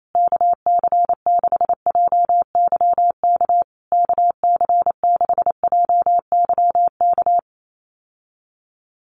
Morse code, heard as a keyed tone, about 28 wpm, 700 Hz. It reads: KC6JYK KC6JYK
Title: USA Call Signs Repeated 3x 28wpm